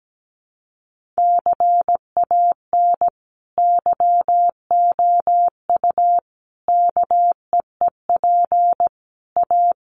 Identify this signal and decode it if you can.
Morse code, heard as a keyed tone, about 17 wpm, 700 Hz. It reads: CAN YOU KEEP A
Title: Sentences from Top 1200 Words 17wpm